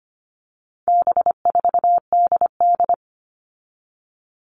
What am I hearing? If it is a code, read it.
Morse code, heard as a keyed tone, about 25 wpm, 700 Hz. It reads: B4DD
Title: All Cut Numbers - 4-Digit 25wpm